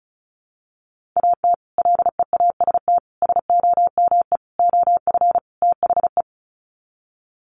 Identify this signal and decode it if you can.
Morse code, heard as a keyed tone, about 34 wpm, 700 Hz. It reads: AT LEAST SOME OF THE
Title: Sets of 5 Words Encore 35wpm